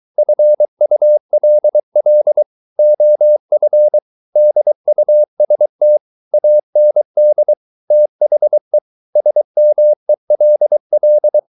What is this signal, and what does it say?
Morse code, heard as a keyed tone, about 23 wpm, 600 Hz. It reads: FULL OF DUST AND THE SMELL